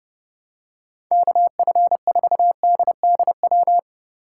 Morse code, heard as a keyed tone, about 30 wpm, 700 Hz. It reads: KF4DDW
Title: USA Call Signs - 6 Characters - 4x Character Spacing - 30wpm